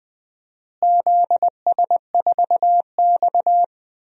Morse code, heard as a keyed tone, about 20 wpm, 700 Hz. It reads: ZS4X
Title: DXCC-100 Call Signs - 4 Characters - Repeated 2x - 20wpm